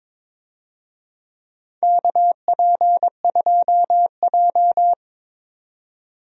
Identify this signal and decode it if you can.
Morse code, heard as a keyed tone, about 21 wpm, 700 Hz. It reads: KP2J
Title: DXCC-100 Call Signs - 4 Characters - 22wpm